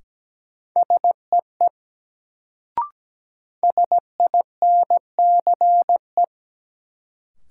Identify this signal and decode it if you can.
Morse code, heard as a keyed tone, about 17 wpm, 700 Hz. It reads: SEE SINCE
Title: Sets of 2 Words - New Word Drill - 17wpm